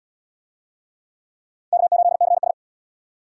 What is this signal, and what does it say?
Morse code, heard as a keyed tone, about 74 wpm, 700 Hz. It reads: K8CS